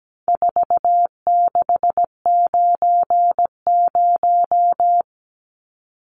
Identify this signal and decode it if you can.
Morse code, heard as a keyed tone, about 17 wpm, 700 Hz. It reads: 4690
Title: Numbers - 4-Digit 17wpm